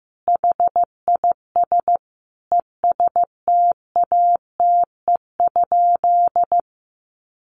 Morse code, heard as a keyed tone, about 15 wpm, 700 Hz. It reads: HIS ESTATE?
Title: Sentences from Top 1200 Words 15wpm